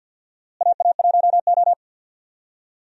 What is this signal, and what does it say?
Morse code, heard as a keyed tone, about 52 wpm, 700 Hz. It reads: AA1O
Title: USA Call Signs - 4 Characters - Repeated 2x - 50wpm